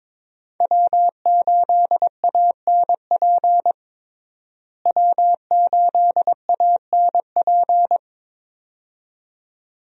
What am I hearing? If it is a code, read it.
Morse code, heard as a keyed tone, about 22 wpm, 700 Hz. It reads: W8ANP W8ANP
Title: USA Call Signs - 5 Characters - Repeated 2x - 22wpm